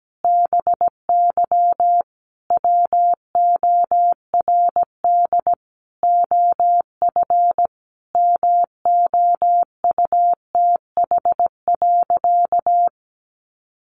Morse code, heard as a keyed tone, about 17 wpm, 700 Hz. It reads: BY WORD OF MOUTH.
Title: Sentences from Top 1500 Words 17wpm